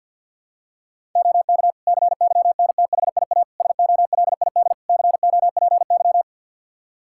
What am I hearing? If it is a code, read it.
Morse code, heard as a keyed tone, about 52 wpm, 700 Hz. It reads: OK CYNTHIA SOLID COPY
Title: Practice QSO - Exchange 3 - More Info - 50wpm